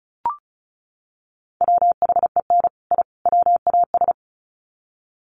Morse code, heard as a keyed tone, about 34 wpm, 700 Hz. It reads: WHEN I WAS
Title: Sets of 3 Words 35wpm